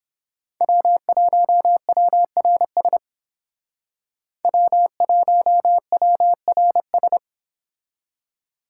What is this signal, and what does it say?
Morse code, heard as a keyed tone, about 29 wpm, 700 Hz. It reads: W1WRS W1WRS